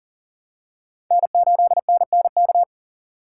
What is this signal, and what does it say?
Morse code, heard as a keyed tone, about 40 wpm, 700 Hz. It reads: N8NNK
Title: USA Call Signs - 2x Character Spacing - 40wpm